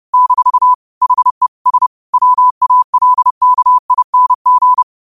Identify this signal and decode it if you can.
Morse code, heard as a keyed tone, about 30 wpm, 1000 Hz. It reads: - HE S WALKING
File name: qotdfiles / 30wpm